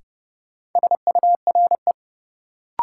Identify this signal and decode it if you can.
Morse code, heard as a keyed tone, about 30 wpm, 700 Hz. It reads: SURE
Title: Sets of 3 Words - New Word Drill - 30wpm